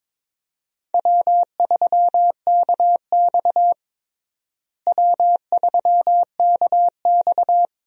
Morse code, heard as a keyed tone, about 22 wpm, 700 Hz. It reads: W3KX W3KX
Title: USA Call Signs - 4 Characters - Repeated 2x - 22wpm